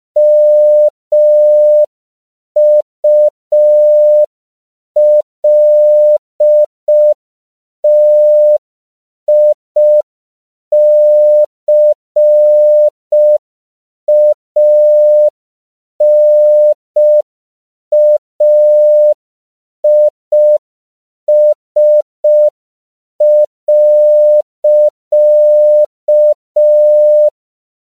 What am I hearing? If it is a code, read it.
Morse code, heard as a keyed tone, about 5 wpm, 600 Hz. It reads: MULTICANAIS.
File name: CW / 5PPM